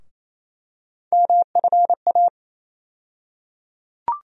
Morse code, heard as a keyed tone, about 28 wpm, 700 Hz. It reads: MFA
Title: USA Call Sign Postfixes - 3 Letters 28wpm